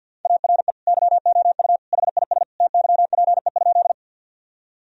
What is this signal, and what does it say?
Morse code, heard as a keyed tone, about 50 wpm, 700 Hz. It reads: ARE YOU HIS TYPE?
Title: Sentences from Top 1000 Words - Review Part 3 - 50wpm